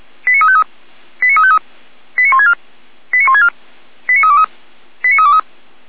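Five-tone selective call (ZVEI), heard as quick runs of five tones, tones of about 70 ms. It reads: 89353 89353 89164 89164 89323 89323
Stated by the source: ZVEI five-tone call 89502 received by SDR.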